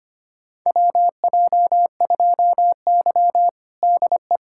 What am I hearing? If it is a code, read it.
Morse code, heard as a keyed tone, about 25 wpm, 700 Hz. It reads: WJ2Y DE
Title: USA Callsign DE Callsign - 4 Characters - 25wpm